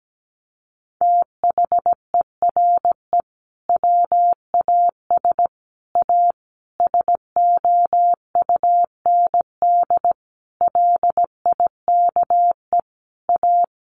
Morse code, heard as a keyed tone, about 17 wpm, 700 Hz. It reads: THERE WAS A SOUND LIKE A
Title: A Sound Of Thunder 17wpm